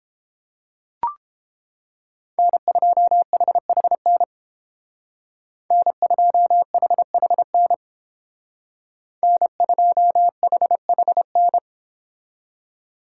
Morse code, heard as a keyed tone, about 31 wpm, 700 Hz. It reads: N2HHN N2HHN N2HHN